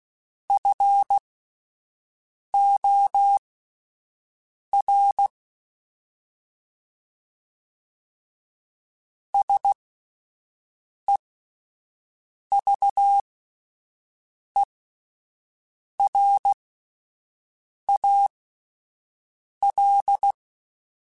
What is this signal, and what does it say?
Morse code, heard as a keyed tone, about 16 wpm, spaced as for 5 wpm, 800 Hz. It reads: FOR SEVERAL